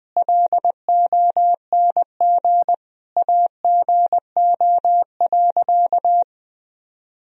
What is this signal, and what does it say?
Morse code, heard as a keyed tone, about 20 wpm, 700 Hz. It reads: LONG AGO.
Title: Sentences from Top 1000 Words - Review Part 3 - 20wpm